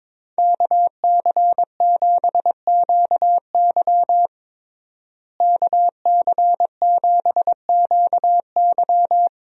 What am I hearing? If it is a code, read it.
Morse code, heard as a keyed tone, about 22 wpm, 700 Hz. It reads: KC7QY KC7QY
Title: USA Call Signs - 5 Characters - Repeated 2x - 22wpm